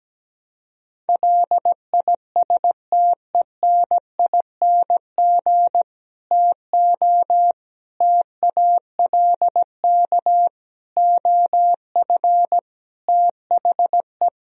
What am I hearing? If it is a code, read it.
Morse code, heard as a keyed tone, about 17 wpm, 700 Hz. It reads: LISTENING TO TALK OF THE
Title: Speed-Racing - Sets of 5 Words 17wpm